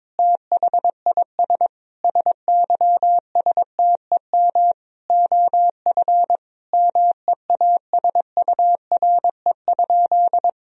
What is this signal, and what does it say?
Morse code, heard as a keyed tone, about 22 wpm, 700 Hz. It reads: THIS SYSTEM OF MEASURE?
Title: Sentences from Top 1000 Words - Review Part 3 - 22wpm